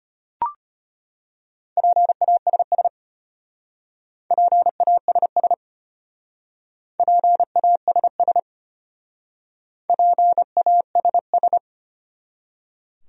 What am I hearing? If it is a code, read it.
Morse code, heard as a keyed tone, about 32 wpm, 700 Hz. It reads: PASS PASS PASS PASS